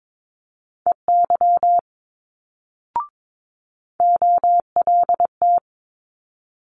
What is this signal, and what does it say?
Morse code, heard as a keyed tone, about 22 wpm, 700 Hz. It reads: EY OLT
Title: Multi-letter Phonemes Quick Fire 22wpm